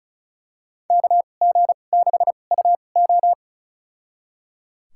Morse code, heard as a keyed tone, about 36 wpm, spaced as for 25 wpm, 700 Hz. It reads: KG6UO
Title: USA Call Signs - 2x Character Spacing - 35wpm